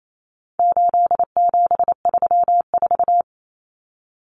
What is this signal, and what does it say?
Morse code, heard as a keyed tone, about 28 wpm, 700 Hz. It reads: 8734
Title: Common Cut Numbers - 4-Digit 28wpm